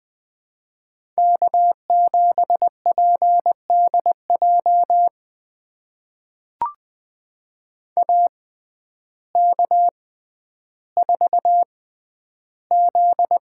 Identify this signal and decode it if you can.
Morse code, heard as a keyed tone, about 20 wpm, spaced as for 11 wpm, 700 Hz. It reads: K7PDJ AK4Z
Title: USA Call Signs - 5 Characters - 6x Character Spacing - 20wpm